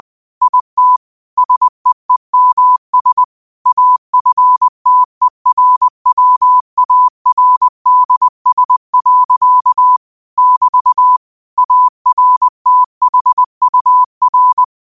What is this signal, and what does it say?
Morse code, heard as a keyed tone, about 20 wpm, 1000 Hz. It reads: IT SEEMS AFTERWARDS. <BT> ARTHUR
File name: qotdfiles / 20wpm